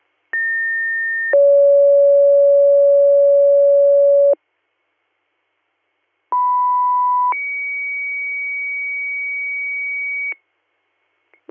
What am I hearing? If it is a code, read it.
Two-tone page, heard as a one-second tone, then a three-second tone, about 1800 then 560 Hz, about 990 then 2250 Hz. It is heard from 0.3 to 10.3 s.